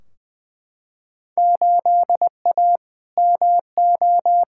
Morse code, heard as a keyed tone, about 20 wpm, 700 Hz. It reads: 8A MO
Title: Field Day Exchanges Encore - 20wpm